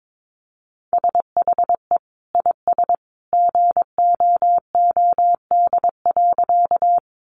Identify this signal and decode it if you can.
Morse code, heard as a keyed tone, about 22 wpm, 700 Hz. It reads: SHE IS GOOD.
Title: Easy Phrases 22wpm